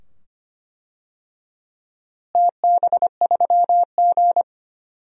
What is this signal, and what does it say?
Morse code, heard as a keyed tone, about 25 wpm, 700 Hz. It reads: TB3G
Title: DXCC-100 Call Signs - 4 Characters - Repeated 2x - 25wpm